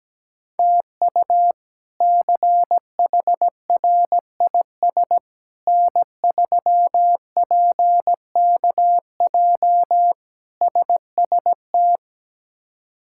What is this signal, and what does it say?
Morse code, heard as a keyed tone, about 17 wpm, 700 Hz. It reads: TU CHRIS N3PKJ SST
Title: K1USN SST QSO Practice - 17wpm